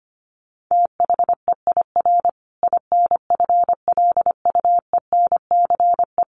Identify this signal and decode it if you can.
Morse code, heard as a keyed tone, about 25 wpm, 700 Hz. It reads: THEIR INFLUENCE
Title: Sentences from Top 1400 Words 25wpm